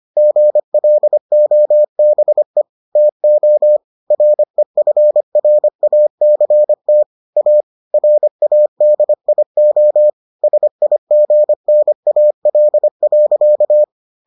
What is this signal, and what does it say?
Morse code, heard as a keyed tone, about 25 wpm, 600 Hz. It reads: GLOBE TO REFRACT A RADIO SIGNAL.